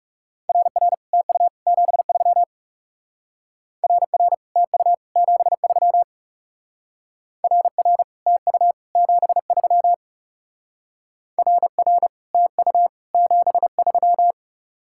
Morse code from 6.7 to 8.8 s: RR TU

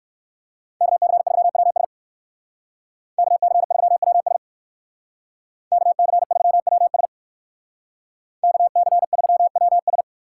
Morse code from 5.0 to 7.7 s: KC3WS

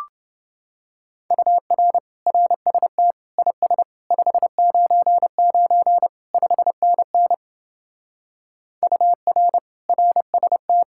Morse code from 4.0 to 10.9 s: 599 5NN UR RST

